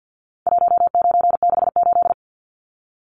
Morse code, heard as a keyed tone, about 50 wpm, 700 Hz. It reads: 1968
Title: Numbers - 4-Digit 50wpm